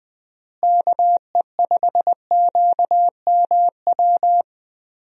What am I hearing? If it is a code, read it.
Morse code, heard as a keyed tone, about 20 wpm, 700 Hz. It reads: KE5QMW
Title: USA Call Signs - 6 Characters - Repeated 3x - 20wpm